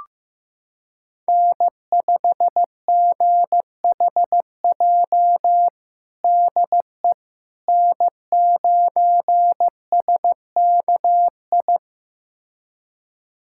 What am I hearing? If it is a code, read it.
Morse code, heard as a keyed tone, about 15 wpm, 700 Hz. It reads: N5GHJ DE N9SKI
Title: USA Callsign DE Callsign - 5 Characters - 15wpm